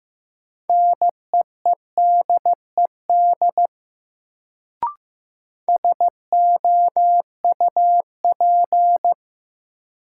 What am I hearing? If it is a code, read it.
Morse code, heard as a keyed tone, about 15 wpm, 700 Hz. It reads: NEEDED SOUP
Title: Sentences from Top 1000 Words - Review Part 6 - New Word Drill - 15wpm